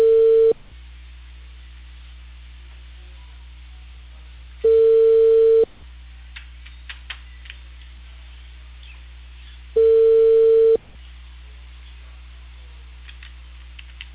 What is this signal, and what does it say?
Ringback tone (China, 450 Hz).